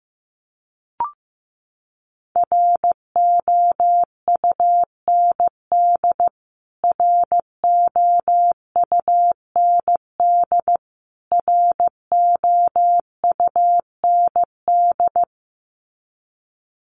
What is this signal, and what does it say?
Morse code, heard as a keyed tone, about 15 wpm, 700 Hz. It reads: ROUND ROUND ROUND
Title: Top 200 Words Repeated 15wpm